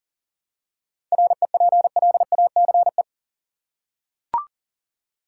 Morse code, heard as a keyed tone, about 40 wpm, 700 Hz. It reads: REPLACE
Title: Seven Letter Words - 40wpm